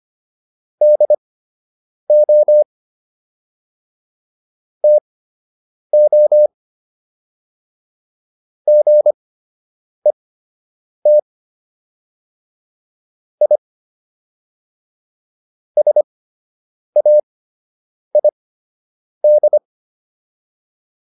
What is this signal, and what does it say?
Morse code, heard as a keyed tone, about 25 wpm, spaced as for 8 wpm, 600 Hz. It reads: DO TO GET I SAID